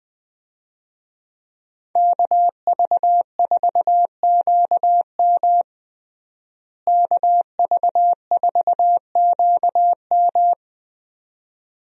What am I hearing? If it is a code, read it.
Morse code, heard as a keyed tone, about 20 wpm, 700 Hz. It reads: KV4QM KV4QM
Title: USA Call Signs - 5 Characters - Repeated 2x - 20wpm